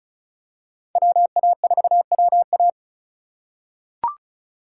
Morse code, heard as a keyed tone, about 36 wpm, 700 Hz. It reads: WA4WA
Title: USA Call Signs - 5 Characters - 3x Character Spacing - 35wpm